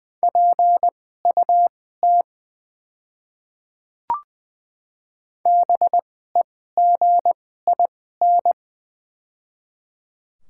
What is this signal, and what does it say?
Morse code, heard as a keyed tone, about 20 wpm, spaced as for 15 wpm, 700 Hz. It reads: PUT BEGIN
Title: 2x Character Spacing - Top 300 Words - 20wpm